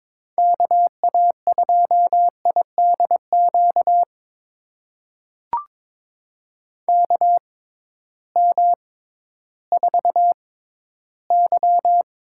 Morse code, heard as a keyed tone, about 22 wpm, spaced as for 13 wpm, 700 Hz. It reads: KA2IDQ KM4Y
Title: USA Call Signs - 6 Characters - 6x Character Spacing - 22wpm